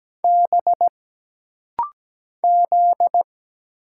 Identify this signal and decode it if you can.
Morse code, heard as a keyed tone, about 17 wpm, 700 Hz. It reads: B Z
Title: Single Letters - Rapid-Fire 17wpm